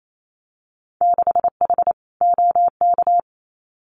Morse code, heard as a keyed tone, about 28 wpm, 700 Hz. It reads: 6H OK
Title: Winter Field Day Exchanges Encore 28wpm